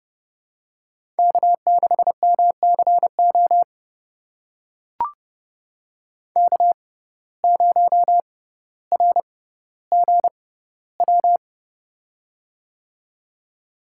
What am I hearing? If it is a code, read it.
Morse code, heard as a keyed tone, about 31 wpm, spaced as for 15 wpm, 690 Hz. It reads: K6MCO K0RGW
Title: USA Call Signs - 5 Characters - 6x Character Spacing - 30wpm